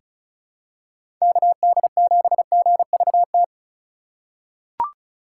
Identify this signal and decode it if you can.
Morse code, heard as a keyed tone, about 36 wpm, 700 Hz. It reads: KD7GVT
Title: USA Call Signs - 6 Characters - 4x Character Spacing - 35wpm